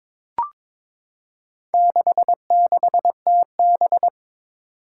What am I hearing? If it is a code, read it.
Morse code, heard as a keyed tone, about 22 wpm, 700 Hz. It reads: 66TB
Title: Four Challenging Letter-Number Combinations 22wpm